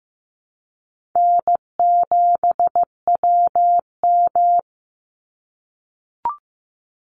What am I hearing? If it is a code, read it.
Morse code, heard as a keyed tone, about 14 wpm, 700 Hz. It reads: N7WM
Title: NA SOTA Chaser Callsigns 15wpm